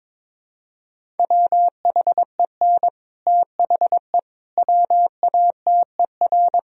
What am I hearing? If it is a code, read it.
Morse code, heard as a keyed tone, about 22 wpm, 700 Hz. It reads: WHEN THE WATER
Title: Sentences from Top 1000 Words - Review Part 1 - 22wpm